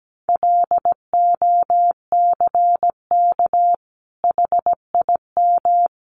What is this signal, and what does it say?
Morse code, heard as a keyed tone, about 17 wpm, 700 Hz. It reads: LOCK HIM
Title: Sentences from Top 1300 Words 17wpm